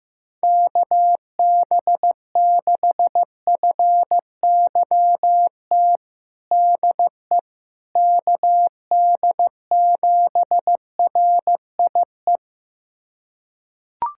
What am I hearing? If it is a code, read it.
Morse code, heard as a keyed tone, about 15 wpm, 700 Hz. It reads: KB6FYT DE KD7RIE
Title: USA Callsign DE Callsign - 6 Characters - 15wpm